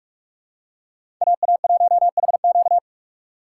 Morse code, heard as a keyed tone, about 41 wpm, 700 Hz. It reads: AA1HQ
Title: USA Call Signs - 5 Characters - 4x Character Spacing - 45wpm